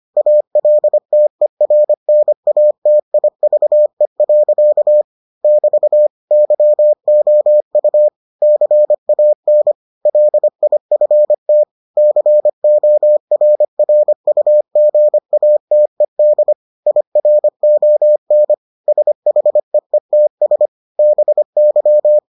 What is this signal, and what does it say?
Morse code, heard as a keyed tone, about 25 wpm, 600 Hz. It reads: ALTERNATIVE. <BT> YOU CAN LIFT CORRUGATED IRON SHEETS BY